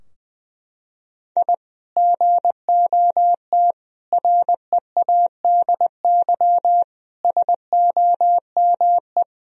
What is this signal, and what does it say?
Morse code, heard as a keyed tone, about 20 wpm, 700 Hz. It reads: I GOT READY SOME
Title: Sentences from Top 1000 Words - Review Part 3 - 20wpm